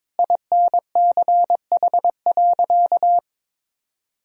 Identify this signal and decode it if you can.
Morse code, heard as a keyed tone, about 22 wpm, 700 Hz. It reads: INCH.
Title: Sentences from Top 1000 Words - Review Part 3 - 22wpm